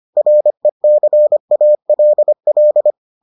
Morse code, heard as a keyed tone, about 25 wpm, 600 Hz. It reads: RECALL